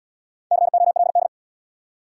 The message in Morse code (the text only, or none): COLD